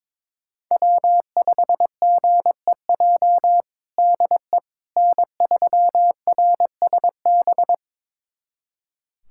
Morse code, heard as a keyed tone, about 22 wpm, 700 Hz. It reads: W5GEJ DE N3RSB
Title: USA Callsign DE Callsign - 5 Characters - 22wpm